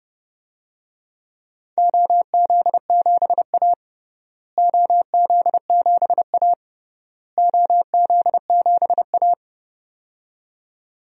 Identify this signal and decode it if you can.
Morse code, heard as a keyed tone, about 31 wpm, 700 Hz. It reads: OZ7A OZ7A OZ7A
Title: DXCC-100 Call Signs - 4 Characters - Repeated 3x - 30wpm